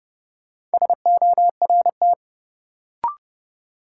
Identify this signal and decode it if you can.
Morse code, heard as a keyed tone, about 29 wpm, 700 Hz. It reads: SORT
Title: Sentences from Top 1300 Words - New Word Drill Alt - 30wpm